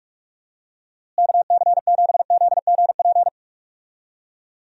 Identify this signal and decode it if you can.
Morse code, heard as a keyed tone, about 46 wpm, 700 Hz. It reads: KC7ZGP
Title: USA Call Signs - 6 Characters - Repeated 2x - 45wpm